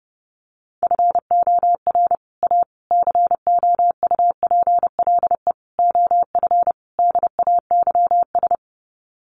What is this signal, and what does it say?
Morse code, heard as a keyed tone, about 30 wpm, 700 Hz. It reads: FOR A COUPLE OF DAYS
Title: Sets of 5 Words Encore2 30wpm